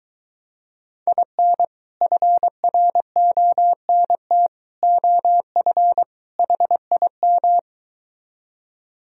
Morse code, heard as a keyed tone, about 23 wpm, 700 Hz. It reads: IN FRONT OF HIM